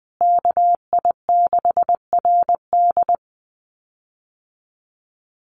Morse code, heard as a keyed tone, about 19 wpm, 700 Hz. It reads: KI6RD
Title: USA Call Signs Repeated 2x 20wpm